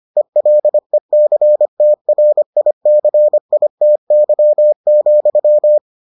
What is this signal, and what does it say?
Morse code, heard as a keyed tone, about 25 wpm, 600 Hz. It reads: ELECTRICITY,